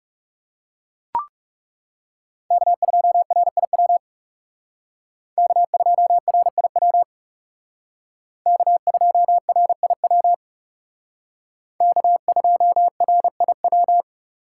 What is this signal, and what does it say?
Morse code, heard as a keyed tone, about 37 wpm, 700 Hz. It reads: K2RIW K2RIW K2RIW K2RIW
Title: Speed Racing - USA Call Signs 30wpm